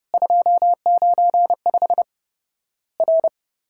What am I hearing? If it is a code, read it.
Morse code, heard as a keyed tone, about 30 wpm, 700 Hz. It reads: 295 R
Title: Worked All Europe DX Contest - QTC Exchange - 30wpm